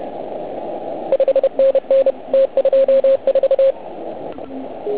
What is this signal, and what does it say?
Morse code, heard as a keyed tone, about 30 wpm, 540 Hz, with radio noise. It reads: 5NN T24
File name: Archiv / CUC129 / CUC129 pileup